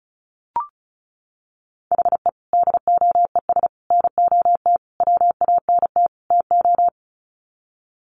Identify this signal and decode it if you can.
Morse code, heard as a keyed tone, about 34 wpm, 700 Hz. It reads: HE DOES NOT WANT TO
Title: Sets of 5 Words 35wpm